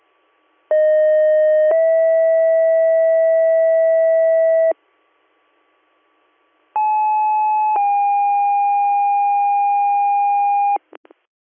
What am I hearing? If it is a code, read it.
Two-tone page, heard as a one-second tone, then a three-second tone, about 620 then 650 Hz, about 860 then 810 Hz. It is heard from 0.7 to 10.8 s.